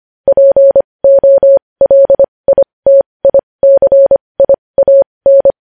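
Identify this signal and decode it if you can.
Morse code, heard as a keyed tone, about 25 wpm, spaced as for 20 wpm, 550 Hz. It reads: POLITICIAN
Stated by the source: Morse code practice sending news headlines at 20 wpm.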